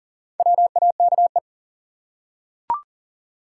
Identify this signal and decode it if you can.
Morse code, heard as a keyed tone, about 40 wpm, 700 Hz. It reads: WAKE
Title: Four Letter Words - 40wpm